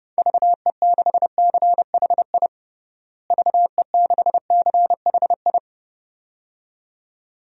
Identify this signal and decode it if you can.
Morse code, heard as a keyed tone, about 30 wpm, 700 Hz. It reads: VE6CHI VE6CHI
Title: DXCC-100 Call Signs - 6 Characters - Repeated 2x - 30wpm